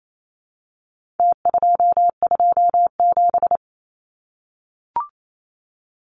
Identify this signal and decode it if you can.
Morse code, heard as a keyed tone, about 28 wpm, 700 Hz. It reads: T227
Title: Numbers - Contest Style - 4-Digit 28wpm